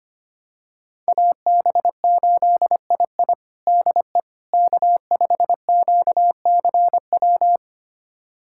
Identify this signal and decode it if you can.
Morse code, heard as a keyed tone, about 25 wpm, 700 Hz. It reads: AB8II DE K5QCW
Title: USA Callsign DE Callsign - 5 Characters - 25wpm